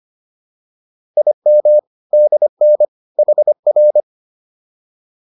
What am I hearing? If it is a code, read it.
Morse code, heard as a keyed tone, about 25 wpm, 600 Hz. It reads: IM DN HR